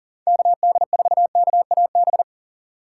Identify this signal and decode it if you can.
Morse code, heard as a keyed tone, about 40 wpm, 700 Hz. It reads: KD4KAB
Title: USA Call Signs - 6 Characters - 6x Character Spacing - 40wpm